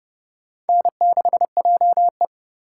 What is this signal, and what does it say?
Morse code, heard as a keyed tone, about 30 wpm, 700 Hz. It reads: N6JE
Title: USA Call Signs - 2x Character Spacing - 30wpm